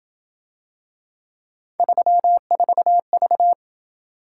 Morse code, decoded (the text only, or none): 34V